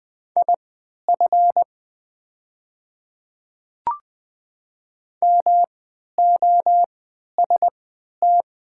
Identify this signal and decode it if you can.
Morse code, heard as a keyed tone, about 20 wpm, spaced as for 11 wpm, 700 Hz. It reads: IF MOST
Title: 3x Character Spacing - Top 100 Words - 20wpm